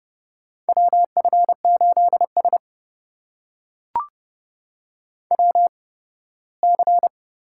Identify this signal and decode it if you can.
Morse code, heard as a keyed tone, about 30 wpm, spaced as for 18 wpm, 700 Hz. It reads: WF8S WC
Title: USA Call Signs - 4 Characters - 8x Character Spacing - 30wpm